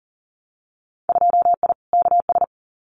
Morse code, heard as a keyed tone, about 38 wpm, 700 Hz. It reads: 2I KS
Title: Winter Field Day Exchanges Encore 40wpm